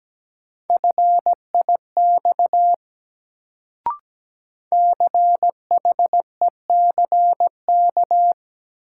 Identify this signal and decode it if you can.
Morse code, heard as a keyed tone, about 17 wpm, 700 Hz. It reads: FIX CHECK
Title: Sentences from Top 1400 Words - New Word Drill Alt - 17wpm